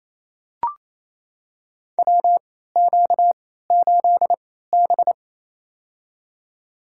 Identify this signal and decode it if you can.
Morse code, heard as a keyed tone, about 28 wpm, spaced as for 16 wpm, 700 Hz. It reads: WQ8B
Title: USA Call Signs - 4 Characters - 3x Character Spacing - 28wpm